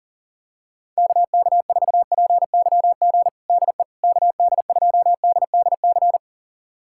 Morse code, heard as a keyed tone, about 40 wpm, 700 Hz. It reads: KK4PYG DE KD2DDC
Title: USA Callsign DE Callsign 40wpm